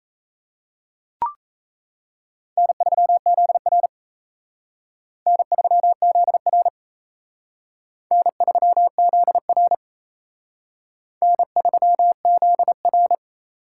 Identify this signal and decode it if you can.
Morse code, heard as a keyed tone, about 35 wpm, 700 Hz. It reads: N3ZR N3ZR N3ZR N3ZR
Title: Speed Racing - USA 1x2 Call Signs 28wpm